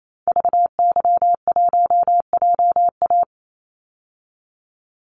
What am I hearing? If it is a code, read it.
Morse code, heard as a keyed tone, about 28 wpm, 700 Hz. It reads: VY1JA
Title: Call Signs Repeated 3x - 28wpm